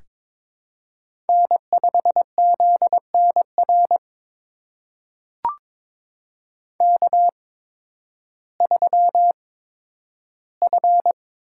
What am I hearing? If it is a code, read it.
Morse code, heard as a keyed tone, about 22 wpm, spaced as for 12 wpm, 700 Hz. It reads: N5ZNR K3F
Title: USA Call Signs - 5 Characters - 8x Character Spacing - 22wpm